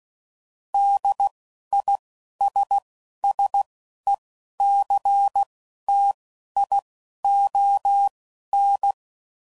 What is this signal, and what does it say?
Morse code, heard as a keyed tone, about 16 wpm, spaced as for 11 wpm, 800 Hz. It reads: DISSECTION